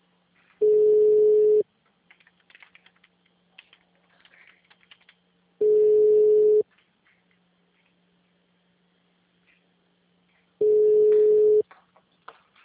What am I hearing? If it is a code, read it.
Ringback tone (Europe, 425 Hz).